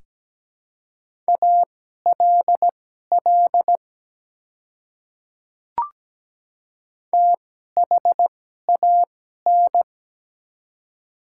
Morse code, heard as a keyed tone, about 17 wpm, spaced as for 12 wpm, 690 Hz. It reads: ALL THAN
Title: 2x Character Spacing - Top 100 Words - 17wpm